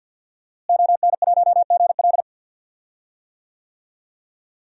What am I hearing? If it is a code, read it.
Morse code, heard as a keyed tone, about 45 wpm, 700 Hz. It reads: ON1GL
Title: DXCC-100 Call Signs - 5 Characters - Repeated 2x - 50wpm